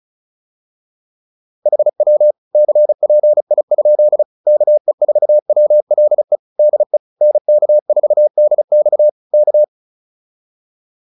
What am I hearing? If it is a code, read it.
Morse code, heard as a keyed tone, about 34 wpm, 600 Hz. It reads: HW CPI? KE4WLE DE NK4DX K